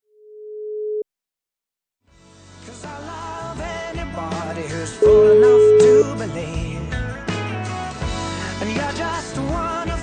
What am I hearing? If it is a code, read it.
Ringback tone (Europe, 425 Hz).